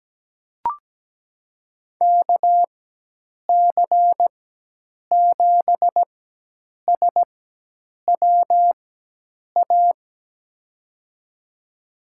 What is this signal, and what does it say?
Morse code, heard as a keyed tone, about 17 wpm, spaced as for 8 wpm, 700 Hz. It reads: KC7SWA
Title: USA Call Signs - 6 Characters - 4x Character Spacing - 17wpm